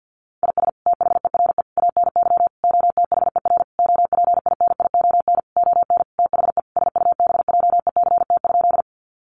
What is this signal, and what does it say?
Morse code, heard as a keyed tone, about 50 wpm, 700 Hz. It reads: IS THERE ANY OTHER OPINION ON THE SUBJECT?
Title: Sentences from Top 1200 Words 50wpm